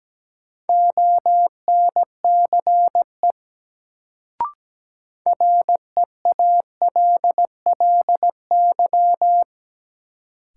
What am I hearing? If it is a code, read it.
Morse code, heard as a keyed tone, about 17 wpm, 700 Hz. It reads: ONCE REALLY
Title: Sets of 4 Words - New Word Drill - 17wpm